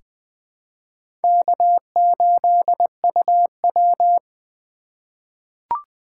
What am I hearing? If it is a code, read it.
Morse code, heard as a keyed tone, about 20 wpm, 700 Hz. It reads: K8UW
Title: USA Call Signs - 4 Characters - 4x Character Spacing - 20wpm